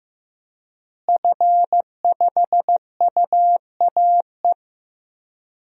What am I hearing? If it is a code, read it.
Morse code, heard as a keyed tone, about 15 wpm, 700 Hz. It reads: F5UAE
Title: DXCC-100 Call Signs - 5 Characters - Repeated 3x - 15wpm